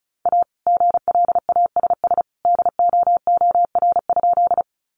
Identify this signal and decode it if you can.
Morse code, heard as a keyed tone, about 34 wpm, 700 Hz. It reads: A GLASS DOOR?
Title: Sentences from Top 1300 Words 35wpm